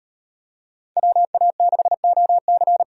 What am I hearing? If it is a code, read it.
Morse code, heard as a keyed tone, about 38 wpm, 700 Hz. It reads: WA6OC